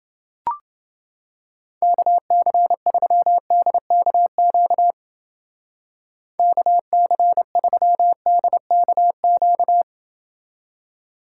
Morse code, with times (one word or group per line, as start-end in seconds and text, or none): 1.8-4.9 KC3DKQ
6.4-9.8 KC3DKQ